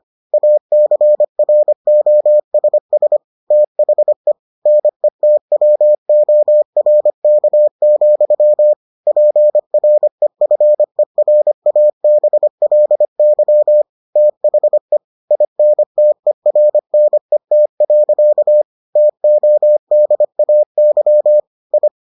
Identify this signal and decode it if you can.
Morse code, heard as a keyed tone, about 25 wpm, 600 Hz. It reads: ACROSS THE NETWORK, PREFERABLY THE INTERNET. TODAY I